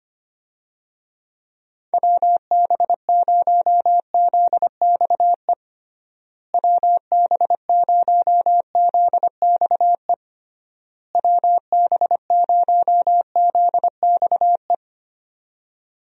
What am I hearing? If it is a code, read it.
Morse code, heard as a keyed tone, about 25 wpm, 700 Hz. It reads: WB0ZXE WB0ZXE WB0ZXE
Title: USA Call Signs - 6 Characters - Repeated 3x - 25wpm